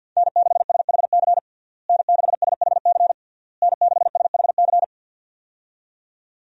Morse code, heard as a keyed tone, about 50 wpm, 700 Hz. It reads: N6SHC N6SHC N6SHC
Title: USA Call Signs - 5 Characters - Repeated 3x - 50wpm